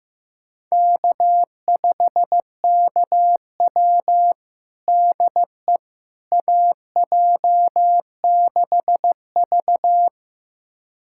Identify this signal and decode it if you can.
Morse code, heard as a keyed tone, about 15 wpm, 700 Hz. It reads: K5KW DE AJ6V
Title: USA Callsign DE Callsign - 4 Characters - 15wpm